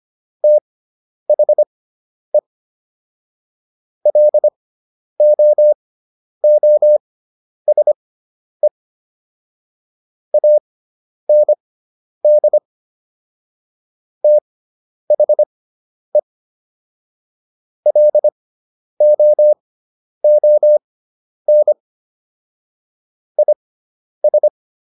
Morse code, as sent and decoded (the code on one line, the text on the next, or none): - .... / .-.. --- --- ... / .- -. -.. / - .... / .-.. --- --- -. / .. ...
THE LOOSE AND THE LOON IS